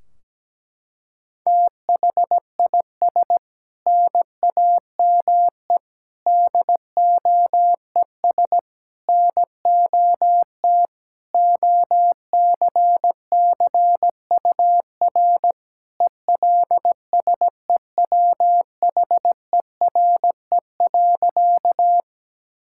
Morse code, heard as a keyed tone, about 17 wpm, 700 Hz. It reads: THIS NAME DOES NOT OCCUR ELSEWHERE.
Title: Sentences from Top 1600 Words 17wpm